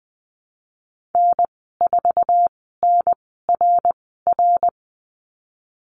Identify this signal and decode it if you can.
Morse code, heard as a keyed tone, about 20 wpm, spaced as for 14 wpm, 700 Hz. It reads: N4NRR
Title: 140 Lesson 23g - Letter f - Call Signs 20wpm